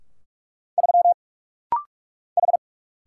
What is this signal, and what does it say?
Morse code, heard as a keyed tone, about 44 wpm, 700 Hz. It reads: 3 H
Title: Single Letter-Number Alt - Rapid-Fire 45wpm